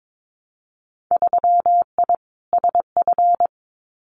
Morse code, heard as a keyed tone, about 22 wpm, 700 Hz. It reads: 3I SF
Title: Winter Field Day Exchanges 22wpm